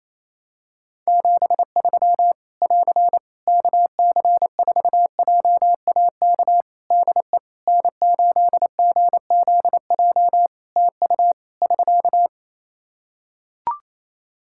Morse code, heard as a keyed tone, about 27 wpm, 700 Hz. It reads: 73 <AR> KC4JAK DE N8GZJ TU <SK>
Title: Practice QSO - Exchange 4 - Ending - 28wpm